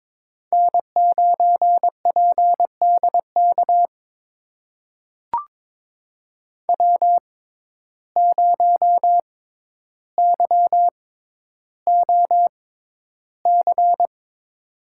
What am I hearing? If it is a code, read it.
Morse code, heard as a keyed tone, about 22 wpm, spaced as for 11 wpm, 700 Hz. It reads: N9PDK W0YOC
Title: USA Call Signs - 6x Character Spacing - 22wpm